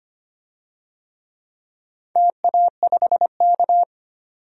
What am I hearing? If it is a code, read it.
Morse code, heard as a keyed tone, about 25 wpm, 700 Hz. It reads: TA5K
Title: DXCC-100 Call Signs - 4 Characters - 25wpm